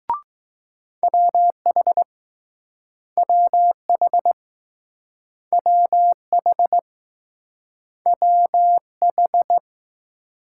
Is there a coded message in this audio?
Yes: Morse code from 1.0 to 9.6 s, heard as a keyed tone, about 20 wpm, 700 Hz. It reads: WH WH WH WH